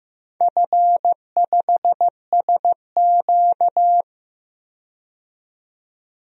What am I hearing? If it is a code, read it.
Morse code, heard as a keyed tone, about 15 wpm, 700 Hz. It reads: F5SQ
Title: DXCC-100 Call Signs - 4 Characters - 15wpm